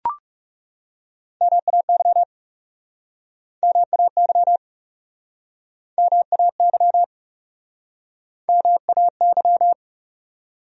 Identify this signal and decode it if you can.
Morse code, heard as a keyed tone, about 40 wpm, 700 Hz. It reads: MAY MAY MAY MAY